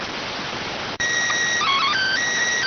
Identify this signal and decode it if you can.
SSTV (slow-scan television), PD 290.